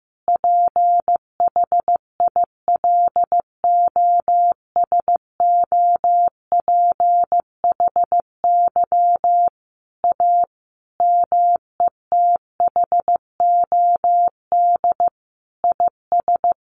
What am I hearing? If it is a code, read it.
Morse code, heard as a keyed tone, about 15 wpm, 700 Hz. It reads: PHILOSOPHY A METHOD IS
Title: Sentences from Top 1400 Words 15wpm